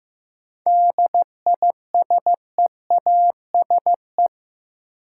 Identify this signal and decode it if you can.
Morse code, heard as a keyed tone, about 15 wpm, 700 Hz. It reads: DISEASE
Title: Sentences from Top 1000 Words - Review Part 6 - New Word Drill - 15wpm